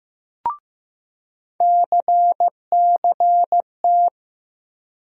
Morse code, heard as a keyed tone, about 16 wpm, 700 Hz. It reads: CCT
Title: USA Call Sign Postfixes - 3 Letters 15wpm